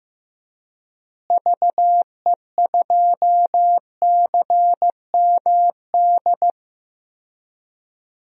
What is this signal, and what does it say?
Morse code, heard as a keyed tone, about 15 wpm, 700 Hz. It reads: VE2CMD
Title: DXCC-100 Call Signs - 6 Characters - Repeated 3x - 15wpm